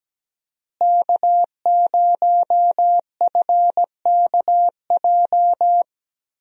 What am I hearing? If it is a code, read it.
Morse code, heard as a keyed tone, about 17 wpm, 700 Hz. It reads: K0FKJ
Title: USA Call Signs - 5 Characters - Repeated 2x - 17wpm